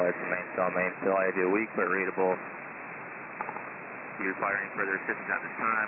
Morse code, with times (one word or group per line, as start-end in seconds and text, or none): none